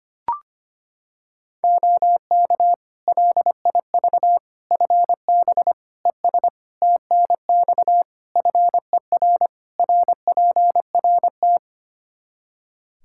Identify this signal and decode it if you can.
Morse code, heard as a keyed tone, about 25 wpm, 700 Hz. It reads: OK LIV FB ES TNX FER RPRT
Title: Practice QSO - Exchange 2 - Info - 25wpm